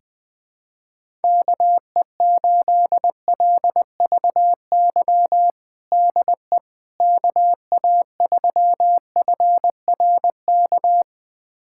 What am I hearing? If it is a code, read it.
Morse code, heard as a keyed tone, about 20 wpm, 700 Hz. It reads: KE8LVY DE KA3FRK
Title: USA Callsign DE Callsign - 6 Characters - 20wpm